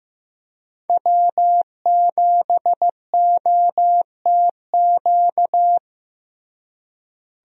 Morse code, heard as a keyed tone, about 15 wpm, 700 Hz. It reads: W7OTQ
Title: USA Call Signs - 5 Characters - Repeated 2x - 15wpm